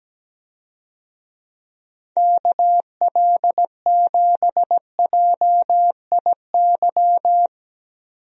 Morse code, heard as a keyed tone, about 17 wpm, 700 Hz. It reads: KL7JIY
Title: USA Call Signs - 6 Characters - 17wpm